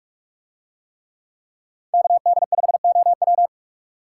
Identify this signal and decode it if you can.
Morse code, heard as a keyed tone, about 44 wpm, 700 Hz. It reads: KD5OW